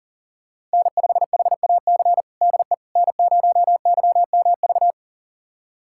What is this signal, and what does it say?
Morse code, heard as a keyed tone, about 40 wpm, 700 Hz. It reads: N5HAC DE N0YMV
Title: USA Callsign DE Callsign - 5 Characters - 40wpm